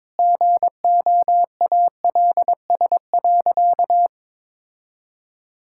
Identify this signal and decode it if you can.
Morse code, heard as a keyed tone, about 22 wpm, 700 Hz. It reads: GOALS.
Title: Sentences from Top 1000 Words - Review Part 6 - 22wpm